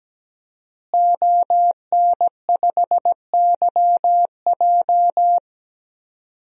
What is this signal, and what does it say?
Morse code, heard as a keyed tone, about 17 wpm, 700 Hz. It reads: ON5YJ
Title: DXCC-100 Call Signs - 5 Characters - Repeated 3x - 17wpm